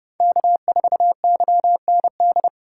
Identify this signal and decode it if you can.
Morse code, heard as a keyed tone, about 31 wpm, 700 Hz. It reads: K4YND
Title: K1USN SST QSO Practice - 30wpm